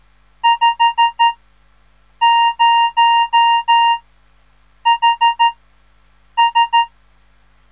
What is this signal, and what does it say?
Morse code, heard as a keyed tone, about 13 wpm, 930 Hz. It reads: 5 0 H S
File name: radiomorningstar / morse / les 3